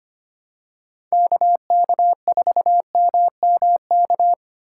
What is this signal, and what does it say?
Morse code, heard as a keyed tone, about 25 wpm, 700 Hz. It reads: KK4MMK
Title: USA Call Signs - 4x Character Spacing - 25wpm